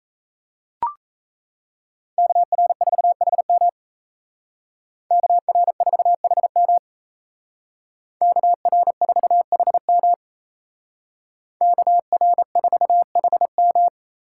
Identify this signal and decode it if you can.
Morse code, heard as a keyed tone, about 34 wpm, 700 Hz. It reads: KR4HM KR4HM KR4HM KR4HM
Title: Speed Racing - USA 2x2 Call Signs 28wpm